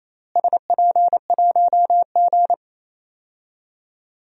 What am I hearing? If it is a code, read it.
Morse code, heard as a keyed tone, about 28 wpm, 700 Hz. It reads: SP1G
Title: DXCC-100 Call Signs - 4 Characters - Repeated 2x - 28wpm